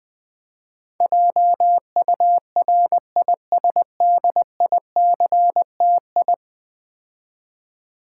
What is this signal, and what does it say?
Morse code, heard as a keyed tone, about 20 wpm, 700 Hz. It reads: JURISDICTI
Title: Build A Word - 5 Letters At A Time - 12-Letter Words - 20wpm